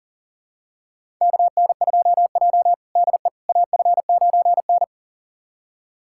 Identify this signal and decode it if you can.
Morse code, heard as a keyed tone, about 40 wpm, 700 Hz. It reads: KN2J DE AF9N
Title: USA Callsign DE Callsign - 4 Characters - 40wpm